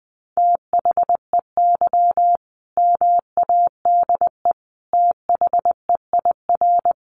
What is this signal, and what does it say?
Morse code, heard as a keyed tone, about 20 wpm, 700 Hz. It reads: THEY MADE THEIR
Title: Sentences from Top 1400 Words 20wpm